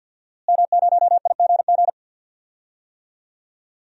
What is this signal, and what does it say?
Morse code, heard as a keyed tone, about 49 wpm, 700 Hz. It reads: M0IGG
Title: DXCC-100 Call Signs - 5 Characters - Repeated 2x - 50wpm